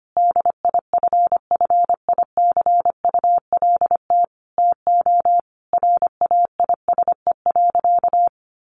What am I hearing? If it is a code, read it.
Morse code, heard as a keyed tone, about 25 wpm, 700 Hz. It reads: DIFFICULT TO RAISE.
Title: Sentences from Top 1400 Words 25wpm